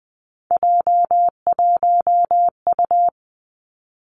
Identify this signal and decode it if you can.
Morse code, heard as a keyed tone, about 20 wpm, 700 Hz. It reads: J1U
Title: Three Challenging Letter-Number Combinations 20wpm